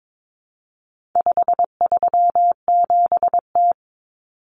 Morse code, heard as a keyed tone, about 22 wpm, 700 Hz. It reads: 537T
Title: Common Cut Numbers - 4-Digit 22wpm